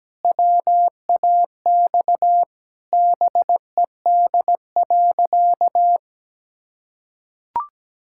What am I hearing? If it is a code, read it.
Morse code, heard as a keyed tone, about 17 wpm, 700 Hz. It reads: WAX BED.
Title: Pangram Challenge v2 - 17wpm